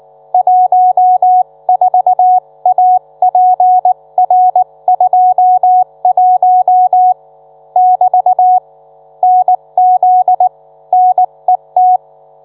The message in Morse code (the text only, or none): 14APR21 <BT> NZ NET